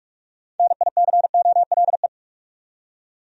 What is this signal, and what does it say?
Morse code, heard as a keyed tone, about 46 wpm, 700 Hz. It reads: NICOLE
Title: USA Common Names - 45wpm